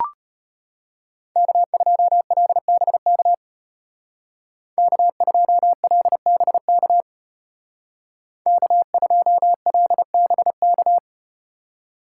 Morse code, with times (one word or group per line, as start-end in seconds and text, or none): 1.4-3.3 K2LBK
4.8-7.0 K2LBK
8.5-11.0 K2LBK